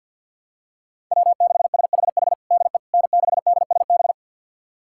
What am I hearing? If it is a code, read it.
Morse code, heard as a keyed tone, about 48 wpm, 700 Hz. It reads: W6SHH DE N6DSB
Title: USA Callsign DE Callsign - 5 Characters - 50wpm